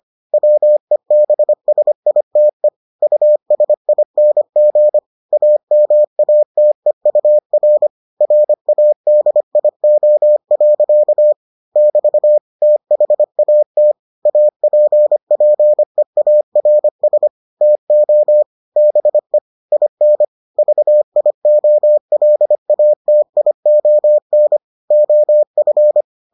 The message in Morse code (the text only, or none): WEBSITE USING AMATEUR RADIO. <BT> THAT APPEARS TO BE IN VIOLATION OF